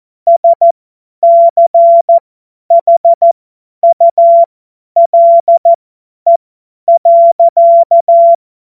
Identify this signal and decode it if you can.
Morse code, heard as a keyed tone, about 14 wpm, spaced as for 10 wpm, 680 Hz. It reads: SCHULE.
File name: Morsekurs / Witze / Witz 10. Januar 2023 50 F70